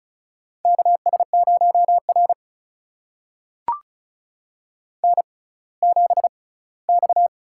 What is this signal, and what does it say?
Morse code, heard as a keyed tone, about 36 wpm, spaced as for 20 wpm, 700 Hz. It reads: KS0R N7X
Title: USA Call Signs - 6x Character Spacing - 35wpm